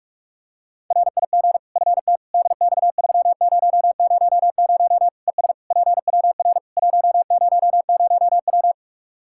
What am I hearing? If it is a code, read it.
Morse code, heard as a keyed tone, about 41 wpm, 700 Hz. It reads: RIG FT DX3000 ES PWR 100W
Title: Practice QSO - Exchange 2 - Info - 45wpm